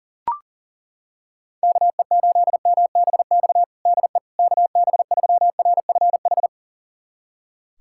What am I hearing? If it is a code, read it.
Morse code, heard as a keyed tone, about 40 wpm, 700 Hz. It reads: KE8MBX DE KB3RFH
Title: USA Callsign DE Callsign 40wpm